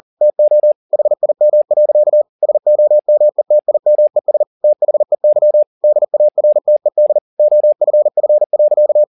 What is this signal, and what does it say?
Morse code, heard as a keyed tone, about 40 wpm, 600 Hz. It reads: TO HIM. SOMETIMES THEY DARTED OFF.
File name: the old man and the sea 2 40wpm